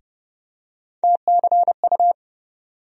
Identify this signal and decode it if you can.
Morse code, heard as a keyed tone, about 30 wpm, 700 Hz. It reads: TCU
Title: USA Call Sign Postfixes - 3 Letters 30wpm